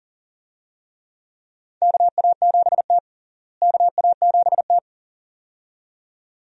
Morse code, heard as a keyed tone, about 37 wpm, 700 Hz. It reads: KA7T KA7T
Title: USA Call Signs - 4 Characters - Repeated 2x - 40wpm